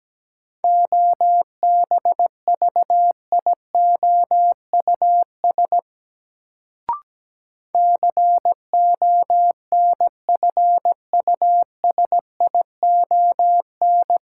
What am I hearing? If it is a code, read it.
Morse code, heard as a keyed tone, about 17 wpm, 700 Hz. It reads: OBVIOUS CONFUSION
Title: Sentences from Top 1600 Words - New Word Drill - 17wpm